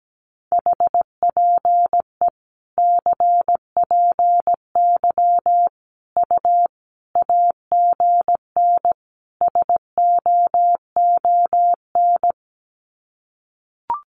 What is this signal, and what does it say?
Morse code, heard as a keyed tone, about 17 wpm, 700 Hz. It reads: HPE CPY U AGN SOON
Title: Common QSO Phrases 17wpm